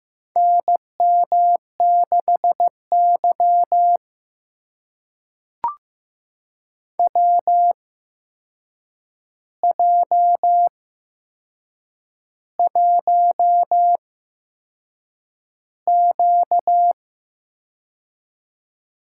Morse code, heard as a keyed tone, about 15 wpm, spaced as for 6 wpm, 700 Hz. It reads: NM6Y WJ1Q
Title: USA Call Signs - 4 Characters - 8x Character Spacing - 15wpm